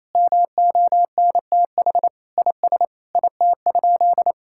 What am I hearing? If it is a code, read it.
Morse code, heard as a keyed tone, about 28 wpm, 700 Hz. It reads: MONTH IS IT?
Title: Sentences from Top 1000 Words - Review Part 5 - 28wpm